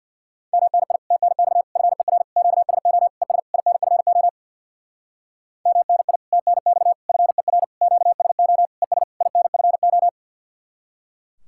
Morse code, heard as a keyed tone, about 64 wpm, 700 Hz. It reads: MNI TNX FER QSO ES INFO MNI TNX FER QSO ES INFO